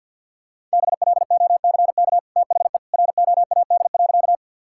Morse code, heard as a keyed tone, about 50 wpm, 700 Hz. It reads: BLOCK THE ROAD.
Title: Sentences from Top 1600 Words 50wpm